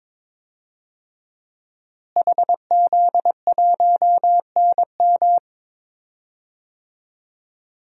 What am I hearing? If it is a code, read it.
Morse code, heard as a keyed tone, about 22 wpm, 700 Hz. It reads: HZ1NM
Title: DXCC-100 Call Signs - 5 Characters - Repeated 2x - 22wpm